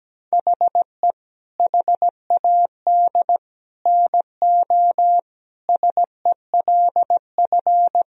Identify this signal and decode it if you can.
Morse code, heard as a keyed tone, about 17 wpm, 700 Hz. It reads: HE HAD NO SELF
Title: Sentences from Top 1000 Words - Review Part 3 - 17wpm